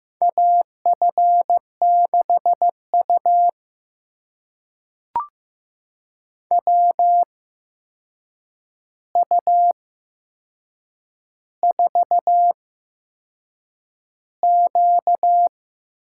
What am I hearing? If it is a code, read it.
Morse code, heard as a keyed tone, about 15 wpm, spaced as for 6 wpm, 700 Hz. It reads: AF6U WU4Q
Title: USA Call Signs - 4 Characters - 8x Character Spacing - 15wpm